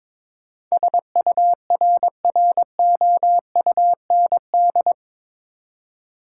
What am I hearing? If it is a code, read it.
Morse code, heard as a keyed tone, about 21 wpm, 700 Hz. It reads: SURROUND
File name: practice / 22wpm / Eight Letter Words - 22wpm